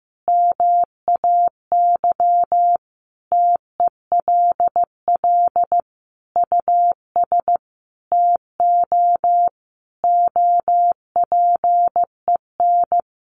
Morse code, heard as a keyed tone, about 15 wpm, 700 Hz. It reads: MAY TELL US TO OPEN
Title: Sentences from Top 300 Words 15wpm